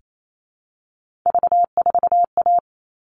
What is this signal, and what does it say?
Morse code, heard as a keyed tone, about 28 wpm, 700 Hz. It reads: V4A
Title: All Cut Numbers - 3-Digit 28wpm